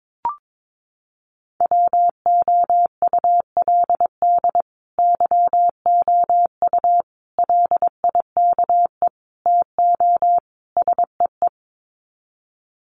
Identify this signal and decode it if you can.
Morse code, heard as a keyed tone, about 22 wpm, 700 Hz. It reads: WOULD YOU LIKE TO SEE
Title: Sets of 5 Words Encore2 22wpm